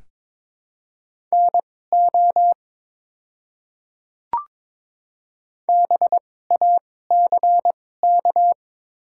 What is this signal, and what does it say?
Morse code, heard as a keyed tone, about 22 wpm, spaced as for 16 wpm, 700 Hz. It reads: NO BACK
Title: 2x Character Spacing - Top 200 Words - 22wpm